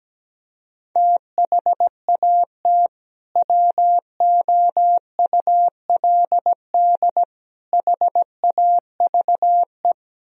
Morse code, heard as a keyed tone, about 17 wpm, 700 Hz. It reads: THAT WOULD HAVE
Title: Sentences from Top 1000 Words - Review Part 3 - 17wpm